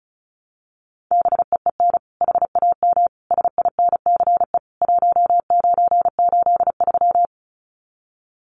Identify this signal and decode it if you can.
Morse code, heard as a keyed tone, about 34 wpm, 700 Hz. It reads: BEEN HAM SINCE 1983
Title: Common QSO Phrases 35wpm